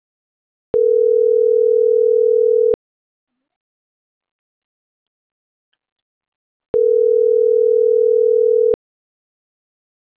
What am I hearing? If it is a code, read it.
Ringback tone (North America), 440 + 480 Hz.